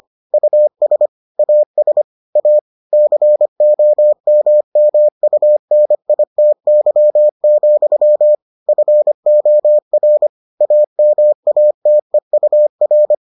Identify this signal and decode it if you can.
Morse code, heard as a keyed tone, about 25 wpm, 600 Hz. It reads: US AS A COMMUNITY, FOR AMATEUR